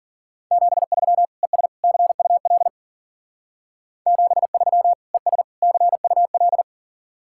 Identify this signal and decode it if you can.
Morse code, heard as a keyed tone, about 51 wpm, 700 Hz. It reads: 73 ES CUL 73 ES CUL